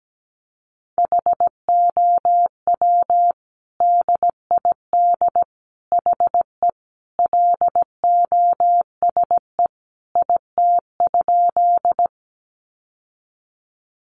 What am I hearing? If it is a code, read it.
Morse code, heard as a keyed tone, about 17 wpm, 700 Hz. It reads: HOW DID HE LOSE IT?
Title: Sentences from Top 1100 Words 17wpm